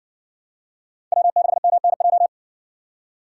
Morse code, heard as a keyed tone, about 58 wpm, 700 Hz. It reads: W6MNJ